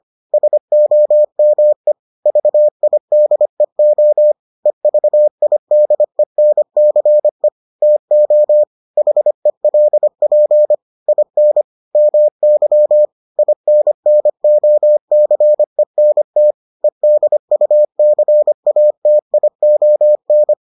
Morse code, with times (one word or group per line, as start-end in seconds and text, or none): 0.3-1.9 SOME
2.3-4.3 VIDEO
4.7-7.5 EVIDENCE
7.8-8.6 TO
9.0-10.7 HELP
11.1-11.6 IN
11.9-13.1 MY
13.4-16.5 INNOCENT
16.8-20.5 EDUCATION